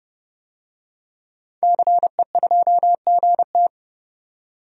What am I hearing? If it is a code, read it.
Morse code, heard as a keyed tone, about 31 wpm, 700 Hz. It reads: CE2GT
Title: DXCC-100 Call Signs - 5 Characters - 30wpm